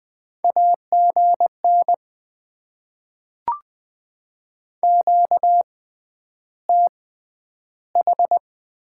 Morse code, heard as a keyed tone, about 20 wpm, spaced as for 10 wpm, 700 Hz. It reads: AGN QTH
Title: Top 20 QSO Elements - 6x Character Spacing - 20wpm